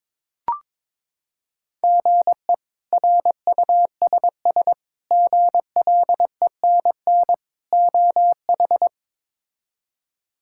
Morse code, heard as a keyed tone, about 22 wpm, 700 Hz. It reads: GE RUSS GLENN OH
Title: K1USN SST QSO Practice - 22wpm